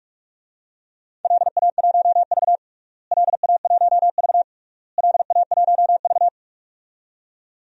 Morse code, heard as a keyed tone, about 46 wpm, 700 Hz. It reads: LA1V LA1V LA1V
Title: DXCC-100 Call Signs - 4 Characters - Repeated 3x - 45wpm